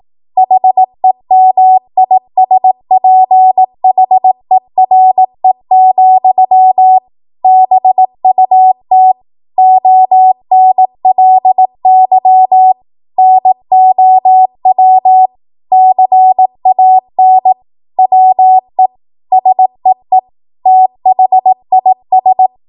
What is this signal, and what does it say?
Morse code, heard as a keyed tone, about 18 wpm, 750 Hz. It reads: HEMISPHERE, BUT ONLY NOW CAN WE SEE THIS
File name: Morse / 18WPM